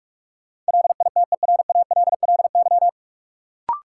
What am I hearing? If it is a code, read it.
Morse code, heard as a keyed tone, about 46 wpm, 700 Hz. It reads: LITERALLY
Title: Sentences from Top 1400 Words - New Word Drill - 45wpm